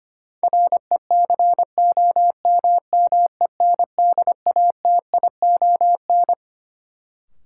Morse code, heard as a keyed tone, about 25 wpm, 700 Hz. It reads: RECOMMENDATION
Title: Sentences from Top 1400 Words - New Word Drill Alt - 25wpm